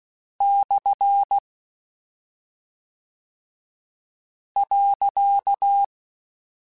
Morse code, heard as a keyed tone, about 16 wpm, 800 Hz. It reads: / .
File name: Misc / Misc-mp3s / Punct @10